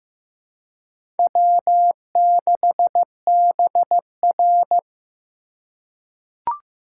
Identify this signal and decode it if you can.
Morse code, heard as a keyed tone, about 15 wpm, 700 Hz. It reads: W6BR
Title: USA Call Signs - 4 Characters - 4x Character Spacing - 15wpm